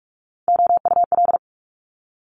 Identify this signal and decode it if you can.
Morse code, heard as a keyed tone, about 46 wpm, 700 Hz. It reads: OUL
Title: Multi-letter Phonemes Quick Fire 45wpm